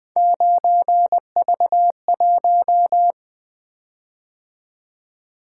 Morse code, heard as a keyed tone, about 20 wpm, 700 Hz. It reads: 9V1
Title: Three Letter-Number Combinations - 20wpm